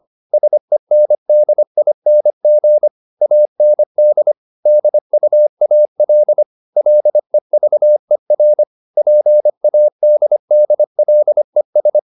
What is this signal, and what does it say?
Morse code, heard as a keyed tone, about 25 wpm, 600 Hz. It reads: SENDING AND DUAL LEVER PADDLES